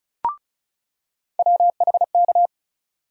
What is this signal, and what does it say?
Morse code, heard as a keyed tone, about 36 wpm, 700 Hz. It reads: WHK
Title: USA Call Sign Postfixes - 3 Letters 35wpm